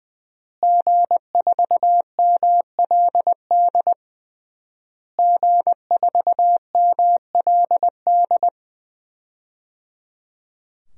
Morse code, heard as a keyed tone, about 20 wpm, 700 Hz. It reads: G4MLD G4MLD
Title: DXCC-100 Call Signs - 5 Characters - Repeated 3x - 20wpm